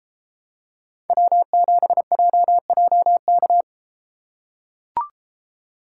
Morse code, heard as a keyed tone, about 32 wpm, 700 Hz. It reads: W7JJK